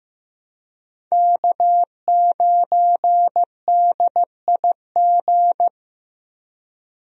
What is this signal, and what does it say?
Morse code, heard as a keyed tone, about 15 wpm, 700 Hz. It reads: K9DIG
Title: USA Call Signs - 5 Characters - Repeated 3x - 15wpm